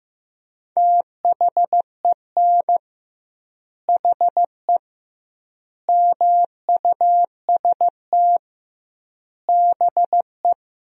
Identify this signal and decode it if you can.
Morse code, heard as a keyed tone, about 15 wpm, 700 Hz. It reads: THEN HE MUST BE
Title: 2x Word-Spacing - Sentences from Top 500 Words - 15wpm